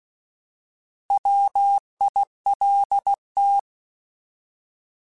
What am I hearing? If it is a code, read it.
Morse code, heard as a keyed tone, about 16 wpm, 800 Hz. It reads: WILT
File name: Alphabetical / Alphabetical mp3s / Ws @15